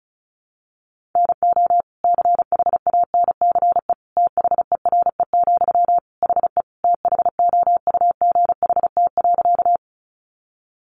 Morse code, heard as a keyed tone, about 34 wpm, 700 Hz. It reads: NO CHANCE THERE, HE THOUGHT.
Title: Sentences from Top 800 Words 35wpm